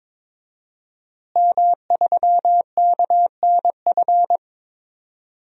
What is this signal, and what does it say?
Morse code, heard as a keyed tone, about 22 wpm, 700 Hz. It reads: M3KNF
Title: DXCC-100 Call Signs - 5 Characters - Repeated 2x - 22wpm